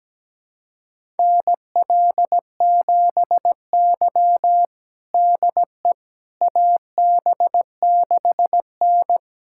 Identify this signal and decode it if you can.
Morse code, heard as a keyed tone, about 17 wpm, 700 Hz. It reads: NL7Y DE AB6N
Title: USA Callsign DE Callsign - 4 Characters - 17wpm